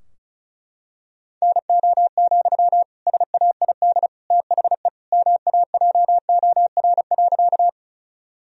Morse code, heard as a keyed tone, about 34 wpm, 700 Hz. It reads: NO, SAID THE MAJOR.
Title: Sentences from Top 1000 Words - Review Part 4 - 35wpm